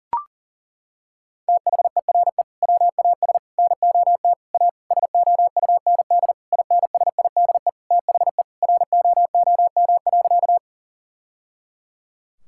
Morse code, heard as a keyed tone, about 37 wpm, 700 Hz. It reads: THERE WAS NOT A SOUND INSIDE THE ROOM.
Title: Sentences from Top 1000 Words - Review Part 6 - 40wpm